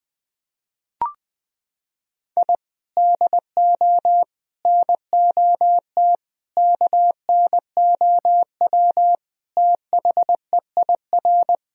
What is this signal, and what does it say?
Morse code, heard as a keyed tone, about 20 wpm, 700 Hz. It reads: I DO NOT KNOW THEIR
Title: Sentences from Top 1000 Words - Review Part 1 - 20wpm